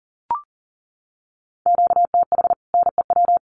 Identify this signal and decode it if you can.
Morse code, heard as a keyed tone, about 40 wpm, 700 Hz. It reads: QTH NEW
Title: Common QSO Phrases Encore 40wpm